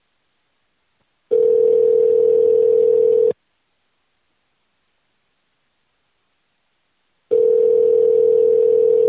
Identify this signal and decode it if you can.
Ringback tone (North America), 440 + 480 Hz.